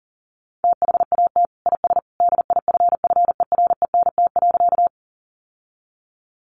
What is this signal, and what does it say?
Morse code, heard as a keyed tone, about 40 wpm, 700 Hz. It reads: THAT IS DIFFERENT.
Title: Sentences from Top 600 Words 40wpm